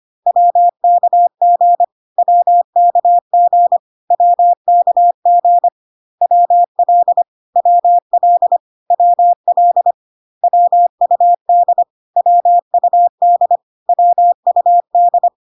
Morse code, heard as a keyed tone, about 25 wpm, 700 Hz. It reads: WKG WKG WKG WL WL WL WUD WUD WUD
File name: CW abbreviations alphabetical repeated 3 times 25 25wpm